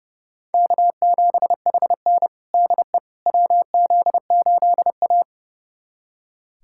Morse code, heard as a keyed tone, about 30 wpm, 700 Hz. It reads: K7HN DE WZ8A
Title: USA Callsign DE Callsign - 4 Characters - 30wpm